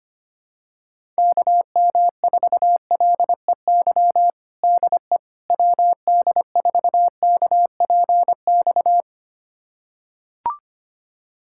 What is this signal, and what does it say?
Morse code, heard as a keyed tone, about 25 wpm, 700 Hz. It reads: KM4LEY DE WD4KPX
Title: USA Callsign DE Callsign 25wpm